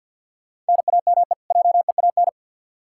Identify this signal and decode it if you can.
Morse code, heard as a keyed tone, about 52 wpm, 700 Hz. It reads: NAME JEAN
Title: Practice QSO - Exchange 1 - Beginning - 50wpm